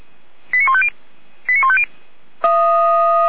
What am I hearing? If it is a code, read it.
Five-tone selective call (ZVEI), heard as quick runs of five tones, tones of about 70 ms. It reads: 89150 89150